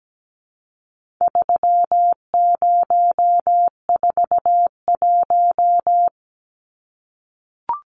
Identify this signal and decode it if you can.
Morse code, heard as a keyed tone, about 17 wpm, 700 Hz. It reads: 3041
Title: Numbers - 4-Digit 17wpm